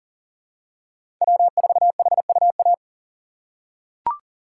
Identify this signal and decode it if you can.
Morse code, heard as a keyed tone, about 40 wpm, 700 Hz. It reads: W4HUA
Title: USA Call Signs - 5 Characters - 3x Character Spacing - 40wpm